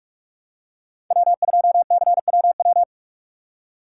Morse code, heard as a keyed tone, about 46 wpm, 700 Hz. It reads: W2CWW
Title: USA Call Signs - 5 Characters - 3x Character Spacing - 45wpm